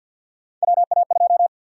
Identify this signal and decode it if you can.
Morse code, heard as a keyed tone, about 52 wpm, 700 Hz. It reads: WA2
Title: USA Call Sign Prefixes - 50wpm